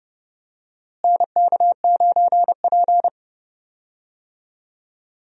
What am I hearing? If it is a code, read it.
Morse code, heard as a keyed tone, about 30 wpm, 700 Hz. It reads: NK9P
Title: USA Call Signs - 4 Characters - Repeated 2x - 30wpm